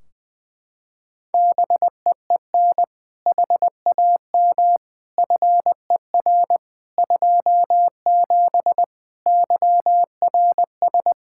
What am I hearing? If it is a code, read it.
Morse code, heard as a keyed tone, about 20 wpm, 700 Hz. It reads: BEEN HAM FER 27 YRS
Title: Practice QSO - Exchange 3 - More Info - 20wpm